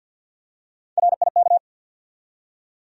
Morse code, heard as a keyed tone, about 54 wpm, 700 Hz. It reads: RIK
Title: USA Call Sign Postfixes - 3 Letters 50wpm